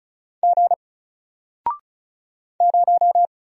Morse code, heard as a keyed tone, about 36 wpm, 700 Hz. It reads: G 0
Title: Single Character (Letter:Number:Punctuation:Prosign) Alt 35wpm